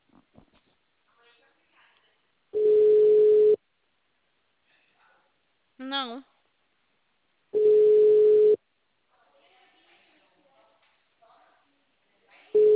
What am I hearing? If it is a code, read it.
Ringback tone (Europe, 425 Hz), with speech.
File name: out- -2013- - - .96971